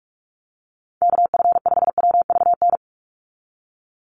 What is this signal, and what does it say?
Morse code, heard as a keyed tone, about 46 wpm, 690 Hz. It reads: KF5WVN
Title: USA Call Signs 45wpm